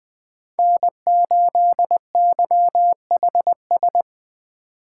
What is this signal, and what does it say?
Morse code, heard as a keyed tone, about 20 wpm, 700 Hz. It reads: N8YHS
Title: Speed Racing - USA 1x3 Call Signs 17wpm